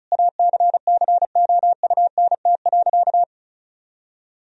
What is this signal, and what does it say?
Morse code, heard as a keyed tone, about 36 wpm, 700 Hz. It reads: ACCOUNT.
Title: Sentences from Top 1000 Words - Review Part 4 - 35wpm